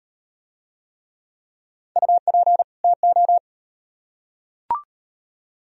Morse code, heard as a keyed tone, about 38 wpm, 700 Hz. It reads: UP TO